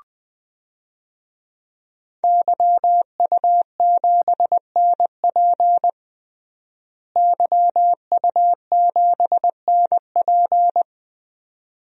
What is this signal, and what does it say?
Morse code, heard as a keyed tone, about 20 wpm, 700 Hz. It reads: YU7NP YU7NP
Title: DXCC-100 Call Signs - 5 Characters - Repeated 2x - 20wpm